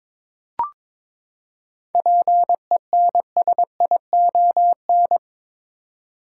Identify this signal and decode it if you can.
Morse code, heard as a keyed tone, about 22 wpm, 700 Hz. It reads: PENSION
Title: Seven Letter Words - 22wpm